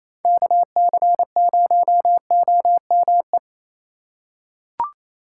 Morse code, heard as a keyed tone, about 28 wpm, 700 Hz. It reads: KC0OME
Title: USA Call Signs - 6 Characters - 6x Character Spacing - 28wpm